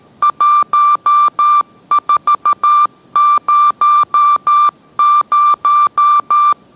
Morse code, heard as a keyed tone, about 13 wpm, 1180 Hz. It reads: 1400